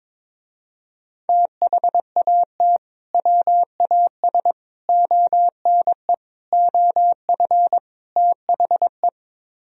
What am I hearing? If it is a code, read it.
Morse code, heard as a keyed tone, about 23 wpm, 700 Hz. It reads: THAT WAS ONE OF THE
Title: Speed-Racing - Sets of 5 Words 22wpm